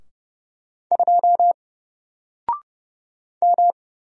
Morse code, heard as a keyed tone, about 30 wpm, 700 Hz. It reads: 2 M
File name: Single-Letter-Number-30wpm